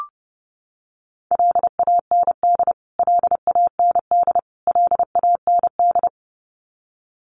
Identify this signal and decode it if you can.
Morse code, heard as a keyed tone, about 30 wpm, 700 Hz. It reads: LAND LAND LAND
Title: Top 500 Words Repeated 30wpm